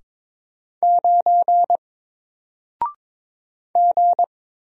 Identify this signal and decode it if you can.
Morse code, heard as a keyed tone, about 23 wpm, 700 Hz. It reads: 9 G
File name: Single-Letter-Number-22wpm